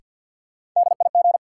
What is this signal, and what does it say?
Morse code, heard as a keyed tone, about 51 wpm, 700 Hz. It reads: DIG
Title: Sentences from Top 1600 Words - New Word Drill - 50wpm